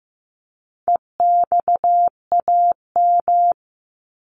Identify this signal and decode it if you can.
Morse code, heard as a keyed tone, about 15 wpm, 700 Hz. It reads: EXAM
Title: Top 501-750 Words 15wpm